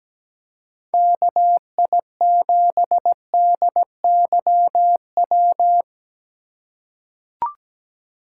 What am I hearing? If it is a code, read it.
Morse code, heard as a keyed tone, about 17 wpm, 700 Hz. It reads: KI7DYW
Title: USA Call Signs - 3x Character Spacing - 17wpm